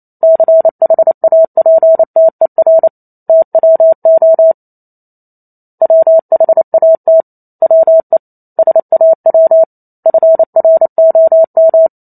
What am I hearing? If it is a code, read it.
Morse code, heard as a keyed tone, about 28 wpm, 650 Hz. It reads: CHAPTER TWO WHAT WE SAW FROM